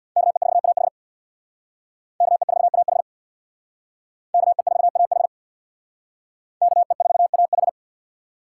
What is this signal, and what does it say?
Morse code, heard as a keyed tone, about 61 wpm, 700 Hz. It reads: KE4AH KE4AH KE4AH KE4AH
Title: Speed Racing - USA 2x2 Call Signs 50wpm